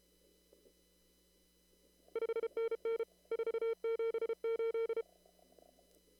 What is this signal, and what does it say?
Morse code, heard as a keyed tone, about 30 wpm, 500 Hz. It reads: 5NN 478
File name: CW / kt5j wpx cw-op-k5pi